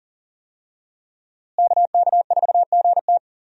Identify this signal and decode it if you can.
Morse code, heard as a keyed tone, about 40 wpm, 700 Hz. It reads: KK4GT
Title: USA Call Signs - 5 Characters - Repeated 2x - 40wpm